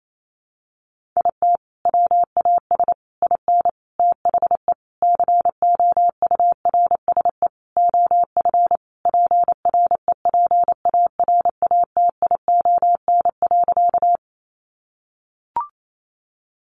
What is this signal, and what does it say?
Morse code, heard as a keyed tone, about 28 wpm, 700 Hz. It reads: IT WAS IN THE COURSE OF PREPARATION.
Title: Sentences from Top 1400 Words 28wpm